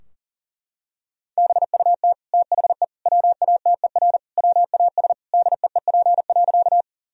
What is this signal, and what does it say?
Morse code, heard as a keyed tone, about 40 wpm, 700 Hz. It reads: BUT THE WATER WAS DEEP.
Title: Sentences from Top 1000 Words - Review Part 3 - 40wpm